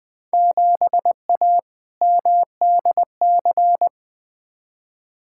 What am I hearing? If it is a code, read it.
Morse code, heard as a keyed tone, about 20 wpm, 700 Hz. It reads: 7A MDC
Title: Field Day Exchanges Encore - 20wpm